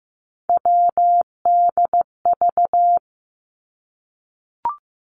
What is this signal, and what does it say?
Morse code, heard as a keyed tone, about 15 wpm, 700 Hz. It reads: WDV
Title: Three Challenging Letter-Number Combinations 15wpm